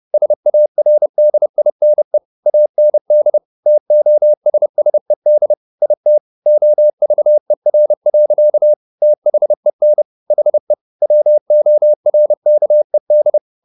Morse code, heard as a keyed tone, about 30 wpm, 600 Hz. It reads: SARDINE AND TOSSED IT OVER. THEN HE WORKED